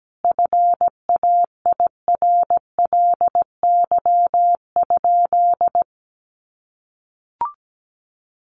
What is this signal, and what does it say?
Morse code, heard as a keyed tone, about 17 wpm, 700 Hz. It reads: FAIRLY?
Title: Sentences from Top 1100 Words 17wpm